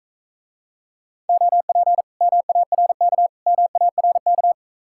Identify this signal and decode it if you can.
Morse code, heard as a keyed tone, about 45 wpm, 700 Hz. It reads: OP MARK MARK